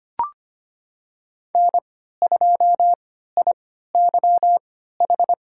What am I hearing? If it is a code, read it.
Morse code, heard as a keyed tone, about 25 wpm, spaced as for 14 wpm, 700 Hz. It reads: N2IYH
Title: USA Call Signs - 3x Character Spacing - 25wpm